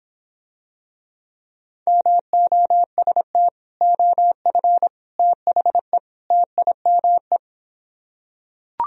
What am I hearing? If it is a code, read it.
Morse code, heard as a keyed tone, about 26 wpm, 700 Hz. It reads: MOST OF THE TIME